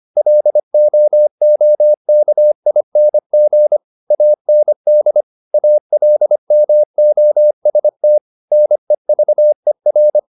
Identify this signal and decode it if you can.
Morse code, heard as a keyed tone, about 25 wpm, 600 Hz. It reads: LOOKING AND ALMOST NEVER
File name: the old man and the sea part 1 25wpm